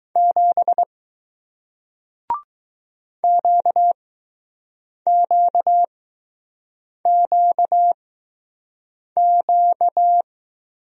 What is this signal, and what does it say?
Morse code, heard as a keyed tone, about 20 wpm, spaced as for 6 wpm, 700 Hz. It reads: 7 QQQQ